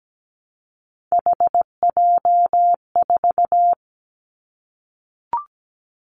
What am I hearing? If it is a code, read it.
Morse code, heard as a keyed tone, about 17 wpm, 700 Hz. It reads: HJ4
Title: Callsign Prefixes for Top 100 DXCC - 17wpm